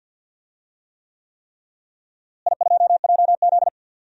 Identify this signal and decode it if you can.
Morse code, heard as a keyed tone, about 49 wpm, 700 Hz. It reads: I2JZ
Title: DXCC-100 Call Signs - 4 Characters - Repeated 3x - 50wpm